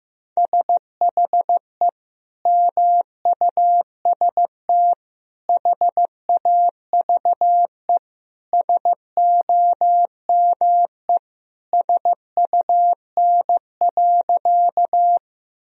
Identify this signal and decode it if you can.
Morse code, heard as a keyed tone, about 15 wpm, 700 Hz. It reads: SHE MUST HAVE SOME SUN.
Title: Sentences from Top 1000 Words - Review Part 1 - 15wpm